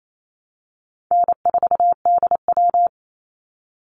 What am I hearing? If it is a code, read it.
Morse code, heard as a keyed tone, about 28 wpm, 700 Hz. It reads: N4DW
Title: Call Signs Repeated 3x - 28wpm